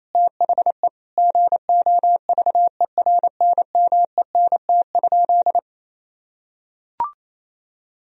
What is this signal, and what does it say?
Morse code, heard as a keyed tone, about 27 wpm, 700 Hz. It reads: THE GOVERNMENT?
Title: Sentences from Top 1000 Words - Review Part 3 - 28wpm